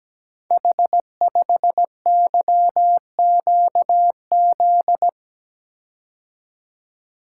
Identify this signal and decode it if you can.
Morse code, heard as a keyed tone, about 17 wpm, 700 Hz. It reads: H5YQZ
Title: Code Groups - 5 Letter:Numbers - 17wpm